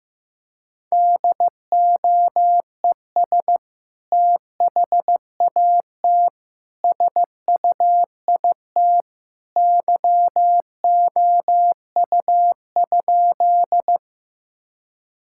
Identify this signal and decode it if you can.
Morse code, heard as a keyed tone, about 15 wpm, 700 Hz. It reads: DOES THAT SUIT YOU?
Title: Sentences from Top 1600 Words 15wpm